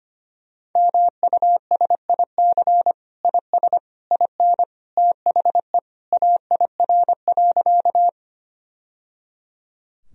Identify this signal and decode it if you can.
Morse code, heard as a keyed tone, about 25 wpm, 700 Hz. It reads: MUSIC IS IN THE AIR.
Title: Sentences from Top 1000 Words - Review Part 1 - 25wpm